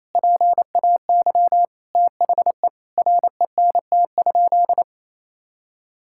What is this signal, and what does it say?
Morse code, heard as a keyed tone, about 27 wpm, 700 Hz. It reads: PAY THE RENT?
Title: Sentences from Top 1000 Words - Review Part 6 - 28wpm